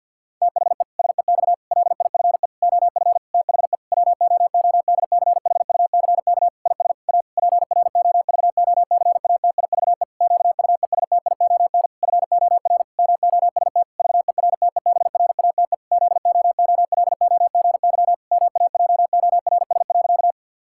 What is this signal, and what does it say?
Morse code, heard as a keyed tone, about 51 wpm, 700 Hz. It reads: THE SEX LIFE OF THE WOODCHUCK IS A PROVOCATIVE QUESTION FOR MOST VERTEBRATE ZOOLOGY MAJORS.
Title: Pangram Challenge v2 - 50wpm